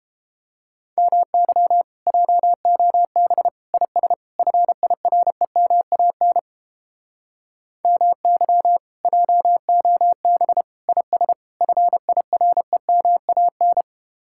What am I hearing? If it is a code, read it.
Morse code, heard as a keyed tone, about 29 wpm, 700 Hz. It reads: MY JOB IS FIREMAN MY JOB IS FIREMAN